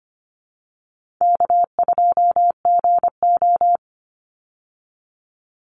Morse code, heard as a keyed tone, about 25 wpm, 700 Hz. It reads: K2GO
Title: Easy Call Signs Repeated 2x - 25wpm